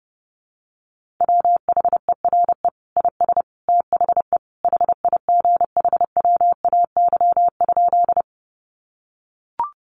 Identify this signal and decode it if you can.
Morse code, heard as a keyed tone, about 30 wpm, 700 Hz. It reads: WHERE IS THE HIGHWAY?
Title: Sentences from Top 1400 Words 30wpm